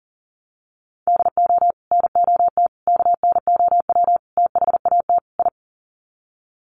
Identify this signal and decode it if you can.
Morse code, heard as a keyed tone, about 37 wpm, 700 Hz. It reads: DO NOT KNOW THAT I
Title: Sets of 5 Words Encore2 40wpm